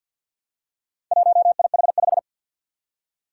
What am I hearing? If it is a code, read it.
Morse code, heard as a keyed tone, about 50 wpm, 700 Hz. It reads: 1IH5
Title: Four Challenging Letter-Number Combinations 50wpm